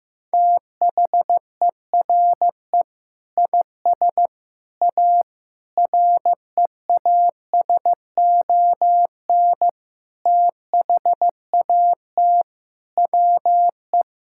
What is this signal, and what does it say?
Morse code, heard as a keyed tone, about 15 wpm, 700 Hz. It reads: THERE IS A REASON THAT WE
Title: Sentences from Top 1000 Words - Review Part 5 - 15wpm